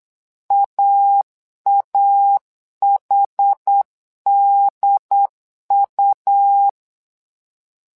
Morse code, heard as a keyed tone, about 8 wpm, 800 Hz. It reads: AAHDU
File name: LEZIONI / 2 UHDN / 2 UHD 8wpm 2258